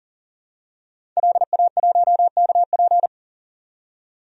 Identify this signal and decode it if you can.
Morse code, heard as a keyed tone, about 40 wpm, 700 Hz. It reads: LA1KP
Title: DXCC-100 Call Signs - 5 Characters - 40wpm